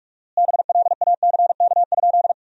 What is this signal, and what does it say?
Morse code, heard as a keyed tone, about 41 wpm, 700 Hz. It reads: BLACK?
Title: 4x Word-Spacing - Sentences from Top 400 Words - 45wpm